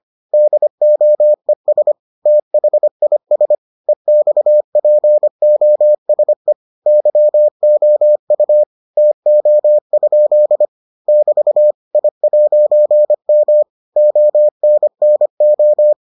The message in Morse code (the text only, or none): DOES THIS EXPOSE YOU TO? <BT> I'M ONNO